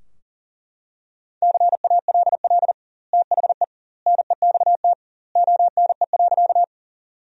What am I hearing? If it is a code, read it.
Morse code, heard as a keyed tone, about 40 wpm, 700 Hz. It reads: CALL THE NEXT ONE.
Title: 2x Word-Spacing - Sentences from Top 300 Words - 40wpm